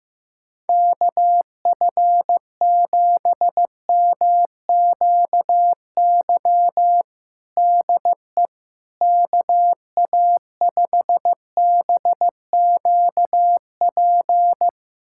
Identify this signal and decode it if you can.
Morse code, heard as a keyed tone, about 15 wpm, 700 Hz. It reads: KF7MQY DE KA5BQP
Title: USA Callsign DE Callsign - 6 Characters - 15wpm